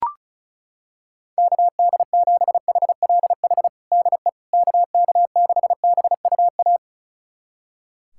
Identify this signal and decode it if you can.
Morse code, heard as a keyed tone, about 34 wpm, 700 Hz. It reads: KD7HLH DE KK6BUA
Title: USA Callsign DE Callsign - 6 Characters - 35wpm